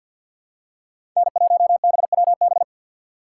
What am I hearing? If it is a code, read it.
Morse code, heard as a keyed tone, about 52 wpm, 700 Hz. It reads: N1BWB
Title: USA Call Signs - 4x Character Spacing - 50wpm